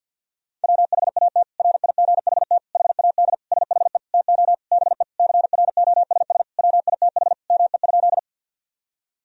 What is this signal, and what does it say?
Morse code, heard as a keyed tone, about 53 wpm, 700 Hz. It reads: WHAT RIGHT HAD SHE TO BE CROSS WITH ME?
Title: Sentences from Top 1000 Words - Review Part 1 - 50wpm